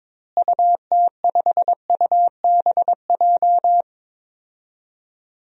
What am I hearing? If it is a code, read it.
Morse code, heard as a keyed tone, about 22 wpm, 700 Hz. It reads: UT5UBJ
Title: DXCC-100 Call Signs - 6 Characters - Repeated 3x - 22wpm